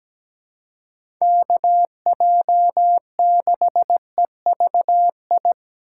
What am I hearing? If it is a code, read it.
Morse code, heard as a keyed tone, about 17 wpm, 700 Hz. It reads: KJ6EVI
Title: USA Call Signs - 6 Characters - 2x Character Spacing - 17wpm